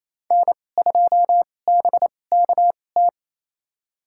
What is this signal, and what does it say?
Morse code, heard as a keyed tone, about 28 wpm, spaced as for 20 wpm, 700 Hz. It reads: N2BKT
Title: USA Call Signs - 5 Characters - 2x Character Spacing - 28wpm